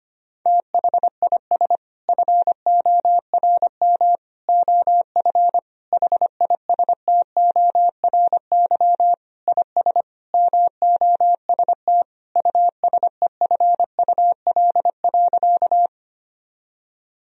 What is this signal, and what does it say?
Morse code, heard as a keyed tone, about 25 wpm, 700 Hz. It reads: THIS FORM OF HISTORY IS MOST USEFUL.
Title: Sentences from Top 1000 Words - Review Part 4 - 25wpm